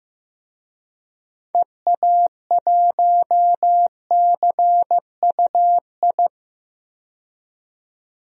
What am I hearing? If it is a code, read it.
Morse code, heard as a keyed tone, about 15 wpm, 700 Hz. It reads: EA1CUI
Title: DXCC-100 Call Signs - 6 Characters - 15wpm